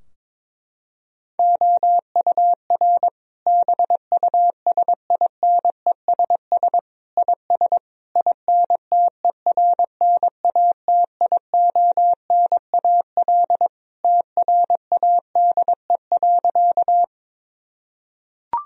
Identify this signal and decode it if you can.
Morse code, heard as a keyed tone, about 22 wpm, 700 Hz. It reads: OUR BUSINESS IS INTERNATIONAL TRADE.
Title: Sentences from Top 1000 Words - Review Part 4 - 22wpm